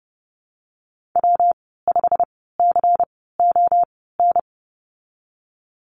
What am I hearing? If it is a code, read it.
Morse code, heard as a keyed tone, about 30 wpm, spaced as for 17 wpm, 700 Hz. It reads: W5CON
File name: Lesson 19g - Letter w - Call Signs 30wpm